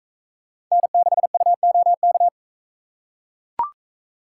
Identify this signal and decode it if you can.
Morse code, heard as a keyed tone, about 40 wpm, 700 Hz. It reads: N6UOK